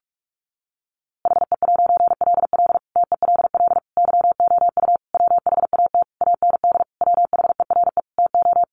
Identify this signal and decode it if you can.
Morse code, heard as a keyed tone, about 44 wpm, 700 Hz. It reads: HE'LL TELL YOU WHAT AND WHERE TO
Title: A Sound Of Thunder 45wpm